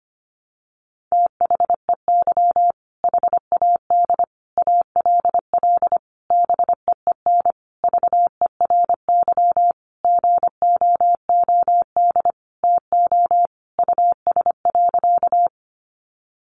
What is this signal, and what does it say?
Morse code, heard as a keyed tone, about 25 wpm, 700 Hz. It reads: THEY HAD ALL BEEN VERY GOOD TO US.
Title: Sentences from Top 200 Words 25wpm